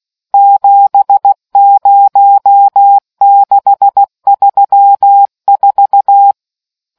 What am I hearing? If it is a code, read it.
Morse code, heard as a keyed tone, about 16 wpm, 800 Hz. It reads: 70634